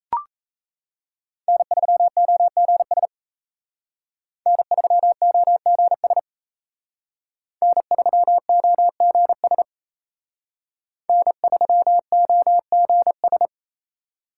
Morse code, heard as a keyed tone, about 36 wpm, 700 Hz. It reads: N3OGS N3OGS N3OGS N3OGS